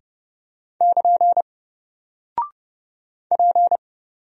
Morse code, heard as a keyed tone, about 30 wpm, 700 Hz. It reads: <KN> P
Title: Single Character (Letter:Number:Punctuation:Prosign) 30wpm